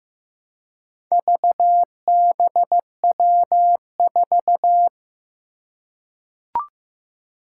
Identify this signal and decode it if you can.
Morse code, heard as a keyed tone, about 15 wpm, 700 Hz. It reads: VBW4
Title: Four Challenging Letter-Number Combinations 15wpm